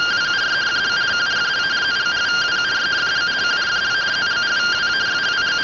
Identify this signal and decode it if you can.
Two-tone teleprinter signal, about 170 Hz shift.